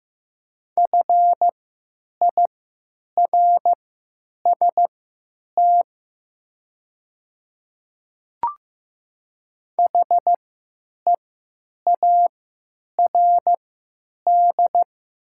Morse code, heard as a keyed tone, about 15 wpm, spaced as for 9 wpm, 700 Hz. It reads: FIRST HEARD
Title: 3x Character Spacing - Top 500 Words - 15wpm